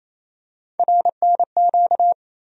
Morse code, heard as a keyed tone, about 27 wpm, 700 Hz. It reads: RNQ
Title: USA Call Sign Postfixes - 3 Letters 28wpm